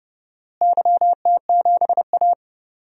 Morse code, heard as a keyed tone, about 30 wpm, 700 Hz. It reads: YT7A
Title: DXCC-100 Call Signs - 4 Characters - Repeated 3x - 30wpm